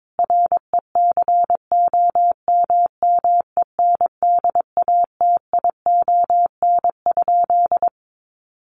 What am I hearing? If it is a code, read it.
Morse code, heard as a keyed tone, about 22 wpm, 700 Hz. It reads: RECOMMENDATION?
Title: Sentences from Top 1400 Words 22wpm